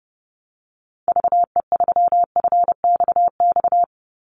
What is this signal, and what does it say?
Morse code, heard as a keyed tone, about 30 wpm, 700 Hz. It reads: VE3FXX
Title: Call Signs Repeated 2x - 30wpm